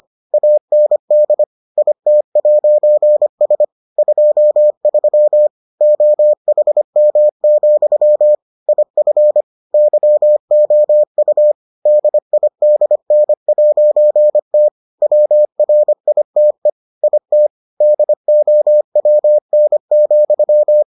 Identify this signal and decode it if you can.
Morse code, heard as a keyed tone, about 25 wpm, 600 Hz. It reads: AND IT'S 23 OHM, IF YOU DIDN'T WRITE IT DOWN,